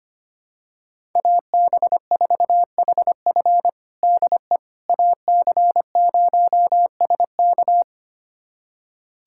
Morse code, heard as a keyed tone, about 25 wpm, 700 Hz. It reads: AB4HF DE AC0SK
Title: USA Callsign DE Callsign - 5 Characters - 25wpm